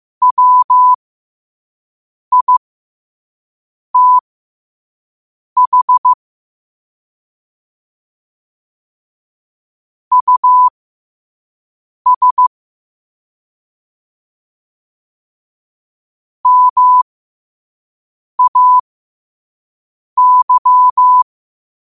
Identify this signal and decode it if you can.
Morse code, heard as a keyed tone, about 15 wpm, spaced as for 5 wpm, 1000 Hz. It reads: WITH US MAY
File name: qotdfiles / 5wpm